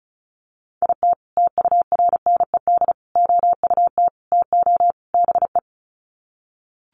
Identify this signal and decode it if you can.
Morse code, heard as a keyed tone, about 34 wpm, 700 Hz. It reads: IT TURNED OUT TO BE
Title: Sets of 5 Words 35wpm